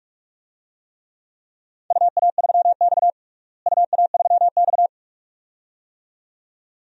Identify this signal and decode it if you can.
Morse code, heard as a keyed tone, about 41 wpm, 700 Hz. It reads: UA3X UA3X
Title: DXCC-100 Call Signs - 4 Characters - Repeated 2x - 45wpm